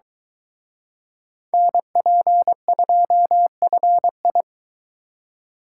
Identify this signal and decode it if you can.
Morse code, heard as a keyed tone, about 24 wpm, 700 Hz. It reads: NP2FI